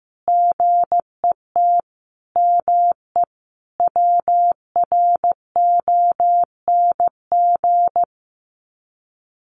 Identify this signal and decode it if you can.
Morse code, heard as a keyed tone, about 15 wpm, 700 Hz. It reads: GET ME WRONG
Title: Sets of 5 Words Encore 15wpm